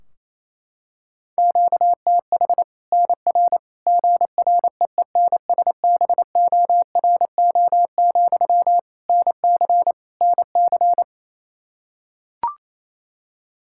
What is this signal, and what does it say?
Morse code, heard as a keyed tone, about 28 wpm, 700 Hz. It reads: QTH NR GREENSBORO, NC NC
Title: Practice QSO - Exchange 1 - Beginning - 28wpm